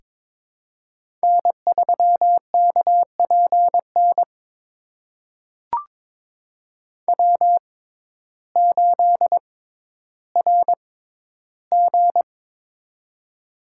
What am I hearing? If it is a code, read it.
Morse code, heard as a keyed tone, about 22 wpm, spaced as for 12 wpm, 700 Hz. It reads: N3KPN W8RG
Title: USA Call Signs - 6x Character Spacing - 22wpm